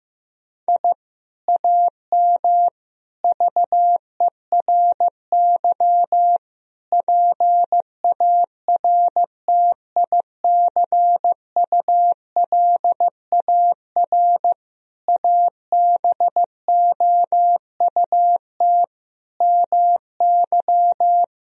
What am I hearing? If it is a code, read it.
Morse code, heard as a keyed tone, about 15 wpm, 700 Hz. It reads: I AM VERY PARTICULAR ABOUT MY
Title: Sentences from Top 1000 Words - Review Part 4 - 15wpm